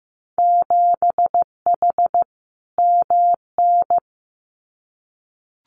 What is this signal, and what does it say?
Morse code, heard as a keyed tone, about 15 wpm, 700 Hz. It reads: 7H MN
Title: Winter Field Day Exchanges Encore 15wpm